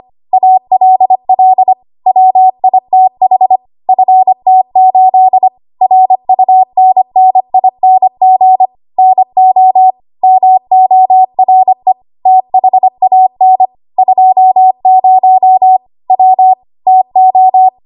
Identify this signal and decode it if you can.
Morse code, heard as a keyed tone, about 25 wpm, 750 Hz. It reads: ALL WITH FT8 RUNNING NO MORE THAN 20 W TO
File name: Archive / 25 WPM / 25WPM